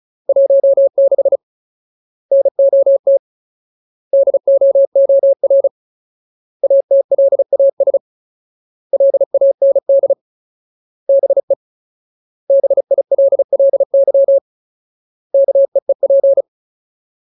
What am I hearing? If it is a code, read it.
Morse code, heard as a keyed tone, about 34 wpm, 550 Hz. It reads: 16 NOT DOOR ATLAS LAND BE BILLY KEEP
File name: Ultimate Word List / 35WPM Ultimate